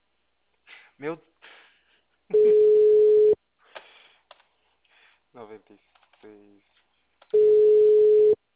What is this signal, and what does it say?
Ringback tone (Europe, 425 Hz), with speech.